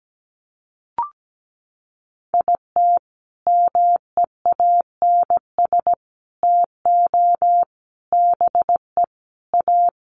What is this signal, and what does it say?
Morse code, heard as a keyed tone, about 17 wpm, 700 Hz. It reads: IT MEANS TO BE A
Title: Sets of 5 Words Encore2 17wpm